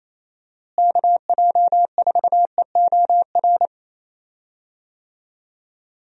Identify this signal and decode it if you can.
Morse code, heard as a keyed tone, about 28 wpm, 700 Hz. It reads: KJ4EOR
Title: USA Call Signs - 6 Characters - Repeated 2x - 28wpm